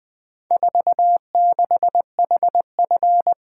Morse code, heard as a keyed tone, about 20 wpm, 700 Hz. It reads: 46HF
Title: Four Letter-Number Combinations - 20wpm